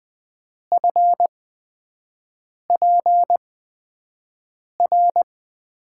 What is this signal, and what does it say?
Morse code, heard as a keyed tone, about 20 wpm, spaced as for 5 wpm, 700 Hz. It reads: FPR
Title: USA Call Signs - 6 Characters - 8x Character Spacing - 20wpm